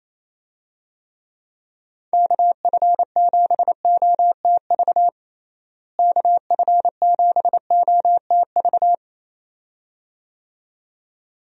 Morse code, heard as a keyed tone, about 27 wpm, 700 Hz. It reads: KF7OTV KF7OTV
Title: USA Call Signs - 6 Characters - Repeated 2x - 28wpm